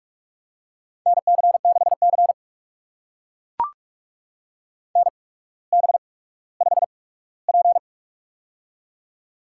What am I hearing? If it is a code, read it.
Morse code, heard as a keyed tone, about 44 wpm, spaced as for 19 wpm, 700 Hz. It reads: NC6C NB5P
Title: USA Call Signs - 4 Characters - 8x Character Spacing - 45wpm